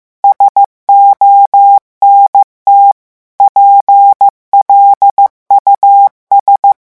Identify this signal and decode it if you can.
Morse code, heard as a keyed tone, about 15 wpm, 800 Hz. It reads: SONT PLUS
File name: morse mp3 / revision9 citations humoristiques vitesse 14 mots minute